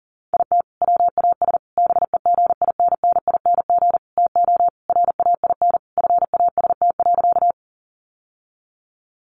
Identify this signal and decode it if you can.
Morse code, heard as a keyed tone, about 40 wpm, 700 Hz. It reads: IT WAS BEGINNING TO RAIN FAST.
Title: Sentences from Top 800 Words 40wpm